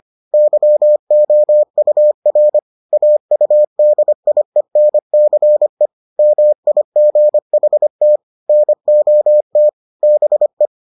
Morse code, heard as a keyed tone, about 25 wpm, 600 Hz. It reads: YOUR AUDIENCE MIGHT NOT BE